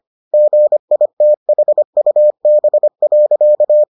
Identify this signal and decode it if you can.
Morse code, heard as a keyed tone, about 25 wpm, 600 Hz. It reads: GITHUB.